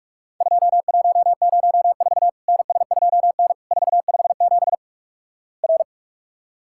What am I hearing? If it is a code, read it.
Morse code, heard as a keyed tone, about 43 wpm, 700 Hz. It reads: 2104 NS2N 457 R
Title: Worked All Europe DX Contest - QTC Exchange - 45wpm